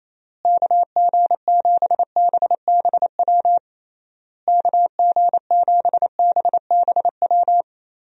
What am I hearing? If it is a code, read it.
Morse code, heard as a keyed tone, about 28 wpm, 700 Hz. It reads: KG7BBW KG7BBW
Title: USA Call Signs - 6 Characters - Repeated 3x - 28wpm